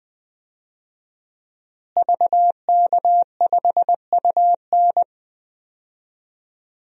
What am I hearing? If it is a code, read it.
Morse code, heard as a keyed tone, about 20 wpm, 700 Hz. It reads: VK5UN
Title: DXCC-100 Call Signs - 5 Characters - Repeated 3x - 20wpm